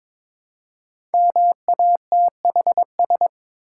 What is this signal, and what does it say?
Morse code, heard as a keyed tone, about 21 wpm, 700 Hz. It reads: MATHS
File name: 5 Letter Words - First 3 Letters Given - 22wpm